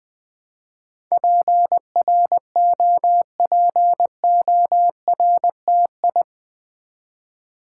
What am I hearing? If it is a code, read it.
Morse code, heard as a keyed tone, about 20 wpm, 700 Hz. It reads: PROPORTI
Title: Build A Word - 4 Letters At A Time - 10-Letter Words - 20wpm